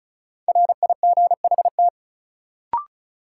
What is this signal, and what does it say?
Morse code, heard as a keyed tone, about 36 wpm, 700 Hz. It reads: RIGHT
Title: Sets of 2 Words - New Word Drill - 35wpm